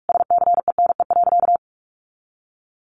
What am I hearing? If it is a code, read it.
Morse code, heard as a keyed tone, about 44 wpm, 700 Hz. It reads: SCENE.
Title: Sentences from Top 1300 Words 45wpm